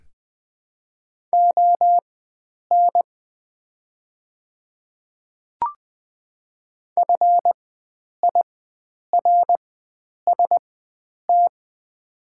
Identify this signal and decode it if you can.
Morse code, heard as a keyed tone, about 20 wpm, spaced as for 9 wpm, 690 Hz. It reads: ON FIRST
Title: 4x Character Spacing - Top 100 Words - 20wpm